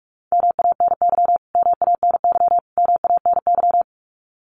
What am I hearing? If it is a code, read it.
Morse code, heard as a keyed tone, about 46 wpm, 700 Hz. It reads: MANY MANY MANY
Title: Top 300 Words Repeated 45wpm